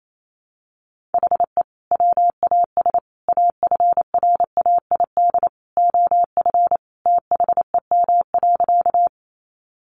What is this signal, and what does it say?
Morse code, heard as a keyed tone, about 28 wpm, 700 Hz. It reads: HE WAS AFRAID OF THEM.
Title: Sentences from Top 1500 Words 28wpm